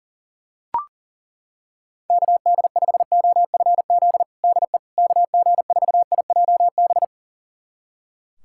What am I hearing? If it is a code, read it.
Morse code, heard as a keyed tone, about 40 wpm, 700 Hz. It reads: KD5OFZ DE KG4IJB
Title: USA Callsign DE Callsign - 6 Characters - 40wpm